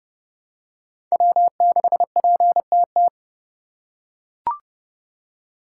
Morse code, heard as a keyed tone, about 32 wpm, 700 Hz. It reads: W6PTT
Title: USA Call Signs - 5 Characters - 2x Character Spacing - 30wpm